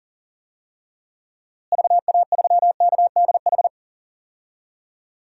Morse code, heard as a keyed tone, about 37 wpm, 700 Hz. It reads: VA3KDH
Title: DXCC-100 Call Signs - 6 Characters - Repeated 2x - 40wpm